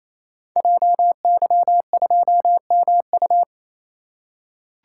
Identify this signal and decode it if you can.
Morse code, heard as a keyed tone, about 27 wpm, 700 Hz. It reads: JY2MU
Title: Code Groups - 5 Letter:Numbers - 28wpm